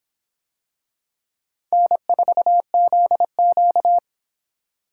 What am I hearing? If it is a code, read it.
Morse code, heard as a keyed tone, about 25 wpm, 700 Hz. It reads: N4ZQ